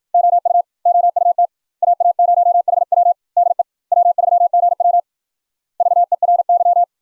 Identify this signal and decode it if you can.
Morse code, heard as a keyed tone, about 60 wpm, 700 Hz. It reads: OU OUT AA0HW DE W3GW VERY